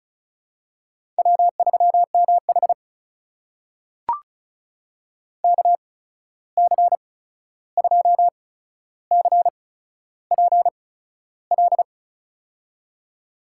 Morse code, heard as a keyed tone, about 36 wpm, spaced as for 13 wpm, 700 Hz. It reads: W3MH KC2CPL
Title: USA Call Signs - 8x Character Spacing - 35wpm